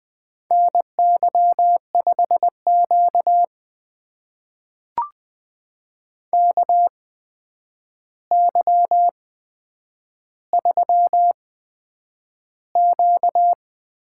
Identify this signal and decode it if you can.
Morse code, heard as a keyed tone, about 20 wpm, spaced as for 9 wpm, 700 Hz. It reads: NY5Q KY3Q
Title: USA Call Signs - 4 Characters - 8x Character Spacing - 20wpm